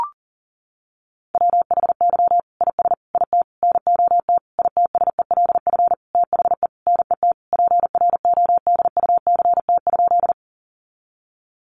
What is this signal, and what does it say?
Morse code, heard as a keyed tone, about 40 wpm, 700 Hz. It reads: WHY IS IT NOT ITSELF THE NET PRODUCT?
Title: Sentences from Top 1400 Words 40wpm